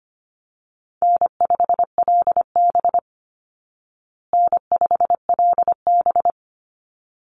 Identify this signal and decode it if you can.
Morse code, heard as a keyed tone, about 25 wpm, 700 Hz. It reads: N5LB N5LB
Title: Call Signs Repeated 3x - 25wpm